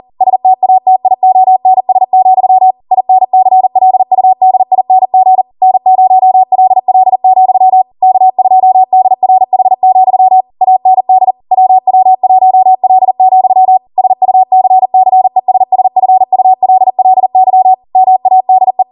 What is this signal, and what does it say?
Morse code, heard as a keyed tone, about 40 wpm, 750 Hz. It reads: STATIONS, INCLUDING N0LL, K2DRH, AND WW1L, SUCCESSFULLY MADE